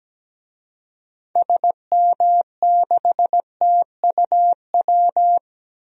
Morse code, heard as a keyed tone, about 17 wpm, 700 Hz. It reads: SM6TUW
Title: DXCC-100 Call Signs - 6 Characters - 17wpm